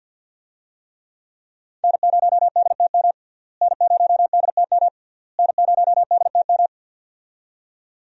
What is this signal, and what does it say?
Morse code, heard as a keyed tone, about 46 wpm, 700 Hz. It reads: N0DTM N0DTM N0DTM
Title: USA Call Signs - 5 Characters - Repeated 3x - 50wpm